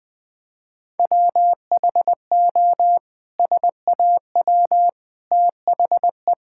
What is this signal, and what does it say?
Morse code, heard as a keyed tone, about 20 wpm, 700 Hz. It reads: WHO SAW THE
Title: Sentences from Top 1600 Words 20wpm